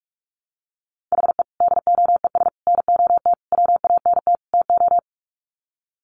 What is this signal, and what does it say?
Morse code, heard as a keyed tone, about 44 wpm, 700 Hz. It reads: HE DOES NOT WANT TO
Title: Sets of 5 Words 45wpm